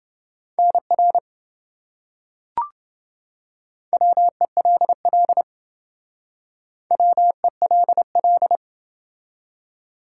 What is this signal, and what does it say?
Morse code, heard as a keyed tone, about 28 wpm, 700 Hz. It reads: NR WELL WELL